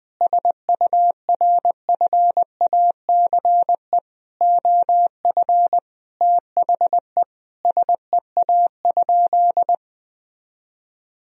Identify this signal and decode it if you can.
Morse code, heard as a keyed tone, about 20 wpm, 700 Hz. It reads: SURFACE OF THE SEA?
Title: Sentences from Top 1000 Words - Review Part 3 - 20wpm